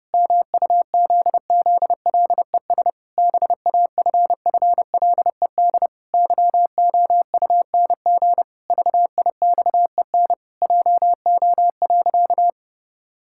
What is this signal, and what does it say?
Morse code, heard as a keyed tone, about 30 wpm, 700 Hz. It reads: MUZZLES BAFFLED YOUNG VIXEN JO.
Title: Pangram Challenge v2 - 30wpm